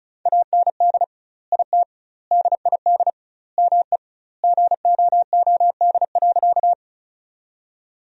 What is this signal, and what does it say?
Morse code, heard as a keyed tone, about 34 wpm, 700 Hz. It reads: AND IT DID ME GOOD.
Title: 2x Word-Spacing - Sentences from Top 200 Words - 35wpm